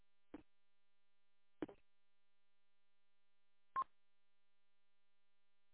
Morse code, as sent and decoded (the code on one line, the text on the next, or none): none